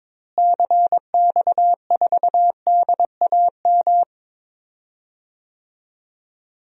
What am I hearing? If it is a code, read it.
Morse code, heard as a keyed tone, about 22 wpm, 700 Hz. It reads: CX4DAM
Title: DXCC-100 Call Signs - 6 Characters - Repeated 2x - 22wpm